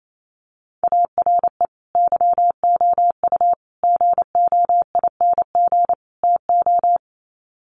Morse code, heard as a keyed tone, about 28 wpm, 700 Hz. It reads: ARE YOU GOING TO
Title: Sets of 4 Words 28wpm